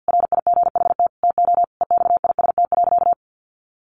Morse code, heard as a keyed tone, about 50 wpm, 700 Hz. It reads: RIGHT TO EXIST.
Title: Sentences from Top 1400 Words 50wpm